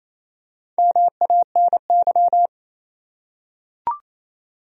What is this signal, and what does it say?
Morse code, heard as a keyed tone, about 28 wpm, 700 Hz. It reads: MANY
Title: Top 100 Words 28wpm